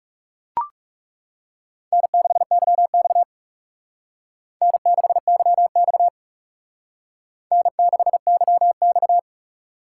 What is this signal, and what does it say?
Morse code, heard as a keyed tone, about 36 wpm, 700 Hz. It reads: N6YX N6YX N6YX
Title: Speed Racing - USA Call Signs 30wpm